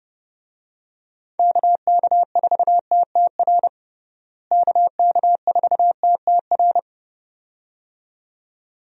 Morse code, heard as a keyed tone, about 32 wpm, 700 Hz. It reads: KK4TTR KK4TTR
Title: USA Call Signs - 6 Characters - Repeated 2x - 30wpm